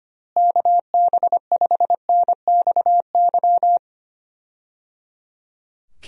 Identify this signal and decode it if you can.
Morse code, heard as a keyed tone, about 25 wpm, 700 Hz. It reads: KB5NXY
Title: USA Call Signs - 6 Characters - Repeated 3x - 25wpm